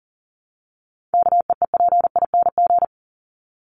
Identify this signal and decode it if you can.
Morse code, heard as a keyed tone, about 40 wpm, 700 Hz. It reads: KEEPING
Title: ING Suffix - Top 100 Words with Base Words 40wpm